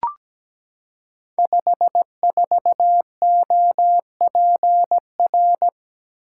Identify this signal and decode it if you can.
Morse code, heard as a keyed tone, about 17 wpm, 700 Hz. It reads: 54OPR
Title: Code Groups - 5 Letter:Numbers - 17wpm